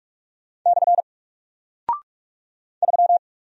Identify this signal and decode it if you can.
Morse code, heard as a keyed tone, about 44 wpm, 700 Hz. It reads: / 3
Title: Single Character (Letter:Number:Punctuation:Prosign) Alt 45wpm